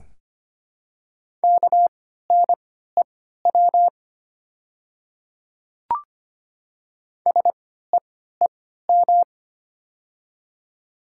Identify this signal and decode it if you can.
Morse code, heard as a keyed tone, about 25 wpm, spaced as for 14 wpm, 700 Hz. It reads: KNEW SEEM
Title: 3x Character Spacing - Top 500 Words - 25wpm